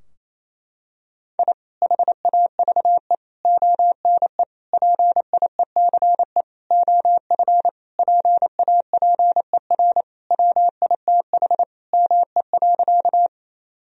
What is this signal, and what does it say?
Morse code, heard as a keyed tone, about 27 wpm, 700 Hz. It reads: I HAVE ONE PIECE OF PAPER WITH ME.
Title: Sentences from Top 1000 Words - Review Part 2 - 28wpm